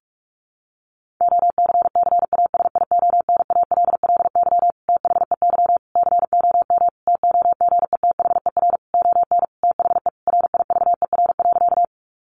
Morse code, heard as a keyed tone, about 44 wpm, 700 Hz. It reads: OCCASIONALLY THEY COM TOGETHER ON THE RIVER.
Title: Sentences from Top 1300 Words 45wpm